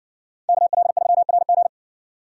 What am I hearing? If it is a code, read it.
Morse code, heard as a keyed tone, about 57 wpm, 700 Hz. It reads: KL3RG